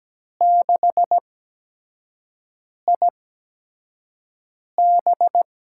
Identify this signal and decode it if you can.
Morse code, heard as a keyed tone, about 17 wpm, spaced as for 5 wpm, 700 Hz. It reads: 6IB
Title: USA Call Signs - 8x Character Spacing - 17wpm